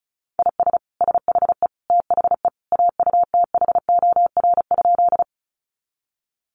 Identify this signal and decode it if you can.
Morse code, heard as a keyed tone, about 34 wpm, 700 Hz. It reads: IS SHE THE AUTHOR?
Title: Sentences from Top 1200 Words 35wpm